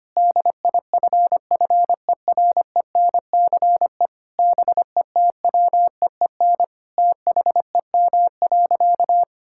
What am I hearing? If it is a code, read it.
Morse code, heard as a keyed tone, about 25 wpm, 700 Hz. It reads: DIFFERENCE BETWEEN THEM.
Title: Sentences from Top 1000 Words - Review Part 5 - 25wpm